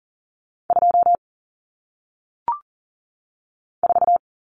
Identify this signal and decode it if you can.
Morse code, heard as a keyed tone, about 39 wpm, 700 Hz. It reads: 2 4
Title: Numbers - 1-Digit 40wpm